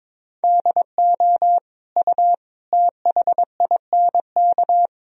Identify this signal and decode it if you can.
Morse code, heard as a keyed tone, about 22 wpm, 700 Hz. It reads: DO U THINK
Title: Sentences from Top 200 Words and Abbreviations 22wpm